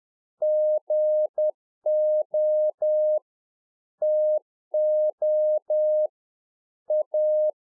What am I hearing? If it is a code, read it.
Morse code, heard as a keyed tone, about 10 wpm, 610 Hz. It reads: GO TO A